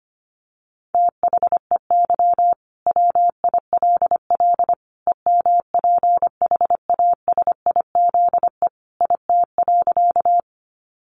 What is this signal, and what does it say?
Morse code, heard as a keyed tone, about 25 wpm, 700 Hz. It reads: THEY WILL EMPHASIZE IT.
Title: Sentences from Top 1500 Words 25wpm